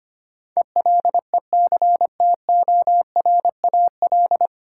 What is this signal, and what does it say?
Morse code, heard as a keyed tone, about 25 wpm, 700 Hz. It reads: ELECTORAL
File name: Nine Letter Words - 25wpm